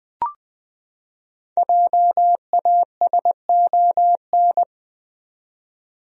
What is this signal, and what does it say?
Morse code, heard as a keyed tone, about 20 wpm, 700 Hz. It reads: JASON
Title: USA Common Names - Max 5 Letters - 20wpm